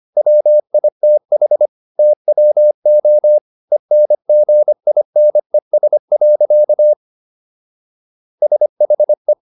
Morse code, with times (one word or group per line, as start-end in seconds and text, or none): 0.2-1.7 WITH
2.0-3.4 TWO
3.7-6.9 ENGINES.
8.4-9.3 SHE